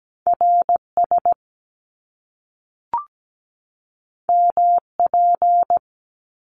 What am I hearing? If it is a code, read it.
Morse code, heard as a keyed tone, about 17 wpm, 700 Hz. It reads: RS MP
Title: Top Two Letter Combinations in Words 17wpm